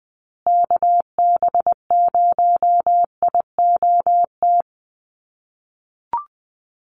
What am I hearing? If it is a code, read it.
Morse code, heard as a keyed tone, about 20 wpm, 700 Hz. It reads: KB0IOT
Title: USA Call Signs Repeated 2x 20wpm